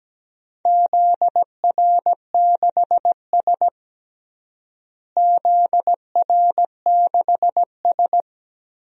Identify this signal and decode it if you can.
Morse code, heard as a keyed tone, about 17 wpm, 700 Hz. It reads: ZR6S ZR6S
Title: DXCC-100 Call Signs - 4 Characters - Repeated 2x - 17wpm